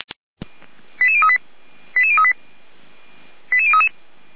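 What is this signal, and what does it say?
Five-tone selective call (ZVEI), heard as quick runs of five tones, tones of about 70 ms. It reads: 88038 88038 88033
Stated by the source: ZVEI five-tone selective call received by SDR.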